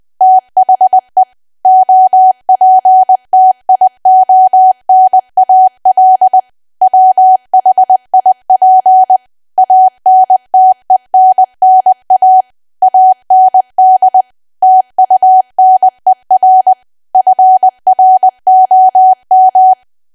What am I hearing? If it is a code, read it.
Morse code, heard as a keyed tone, about 20 wpm, 750 Hz. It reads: THE OPTIONAL WHIP ANTENNA AND TUNER FROM
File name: cwa / W1AW-Level3 / 20WPM